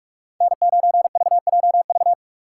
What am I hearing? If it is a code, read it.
Morse code, heard as a keyed tone, about 41 wpm, 700 Hz. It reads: N9VJV
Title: USA Call Signs - 5 Characters - 4x Character Spacing - 45wpm